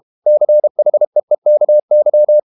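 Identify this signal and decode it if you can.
Morse code, heard as a keyed tone, about 32 wpm, 600 Hz. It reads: CHEEKY
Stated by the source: Morse code practice at 32 wpm from CW med Gnister, reading Ken Follett's novel Never in English.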